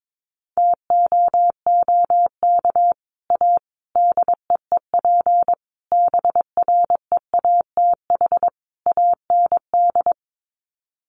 Morse code, heard as a keyed tone, about 22 wpm, 700 Hz. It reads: TOOK A DEEP BREATH AND
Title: Sets of 5 Words Encore 22wpm